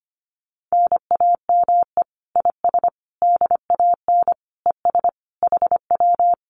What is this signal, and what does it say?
Morse code, heard as a keyed tone, about 25 wpm, 700 Hz. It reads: NAME IS DAN ES HW
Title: Common QSO Phrases Encore 25wpm